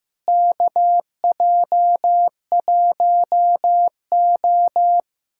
Morse code, heard as a keyed tone, about 15 wpm, 700 Hz. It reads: KJ1O
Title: USA Callsign DE Callsign - 4 Characters - 15wpm